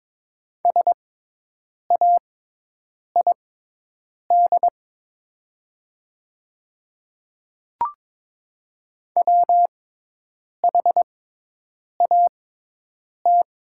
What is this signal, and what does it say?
Morse code, heard as a keyed tone, about 22 wpm, spaced as for 8 wpm, 700 Hz. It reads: SAID WHAT
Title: 6x Character Spacing - Top 100 Words - 22wpm